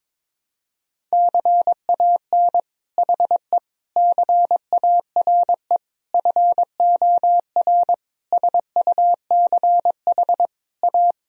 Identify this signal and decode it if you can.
Morse code, heard as a keyed tone, about 22 wpm, 700 Hz. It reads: CAN HE CARE FOR SUCH A
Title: Sentences from Top 1000 Words - Review Part 1 - 22wpm